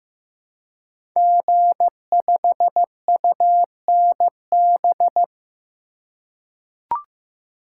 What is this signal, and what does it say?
Morse code, heard as a keyed tone, about 15 wpm, 700 Hz. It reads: G5UNB
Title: Code Groups - 5 Letter:Numbers - 15wpm